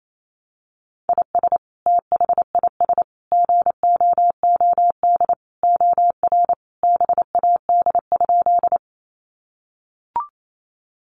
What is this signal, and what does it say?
Morse code, heard as a keyed tone, about 28 wpm, 700 Hz. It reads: IS THIS GOOD OR BAD?
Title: Sentences from Top 700 Words 28wpm